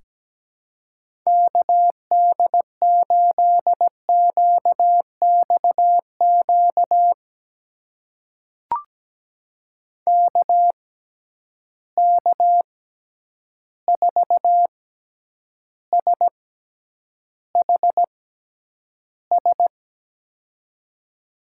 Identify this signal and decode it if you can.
Morse code, heard as a keyed tone, about 17 wpm, spaced as for 9 wpm, 700 Hz. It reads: KD8QXQ KK4SHS
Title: USA Call Signs - 6 Characters - 6x Character Spacing - 17wpm